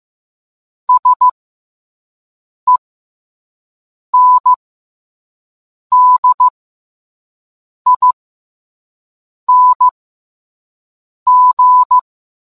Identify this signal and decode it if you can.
Morse code, heard as a keyed tone, about 15 wpm, spaced as for 5 wpm, 1000 Hz. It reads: SENDING